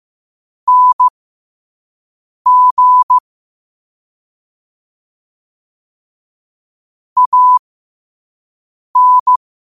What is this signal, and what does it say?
Morse code, heard as a keyed tone, about 15 wpm, spaced as for 5 wpm, 1000 Hz. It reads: NG AN
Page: New quotes every day in morse code at 5 Words per minute.